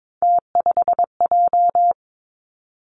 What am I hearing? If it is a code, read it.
Morse code, heard as a keyed tone, about 22 wpm, 700 Hz. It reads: T5J
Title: Three Challenging Letter-Number Combinations 22wpm